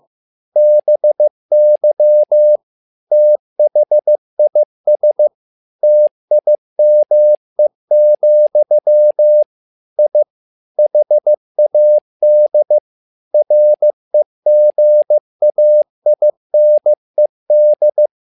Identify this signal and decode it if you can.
Morse code, heard as a keyed tone, about 15 wpm, 600 Hz. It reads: BY THIS TIME, I HAD REGAINED
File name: learning center / tower trouble / tower trouble 15wpm